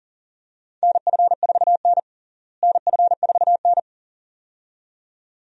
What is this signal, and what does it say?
Morse code, heard as a keyed tone, about 40 wpm, 700 Hz. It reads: NF4N NF4N
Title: USA Call Signs - 4 Characters - Repeated 2x - 40wpm